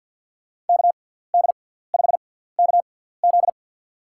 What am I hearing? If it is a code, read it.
Morse code, heard as a keyed tone, about 52 wpm, spaced as for 17 wpm, 700 Hz. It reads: KD5KZ
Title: USA Call Signs - 6x Character Spacing - 50wpm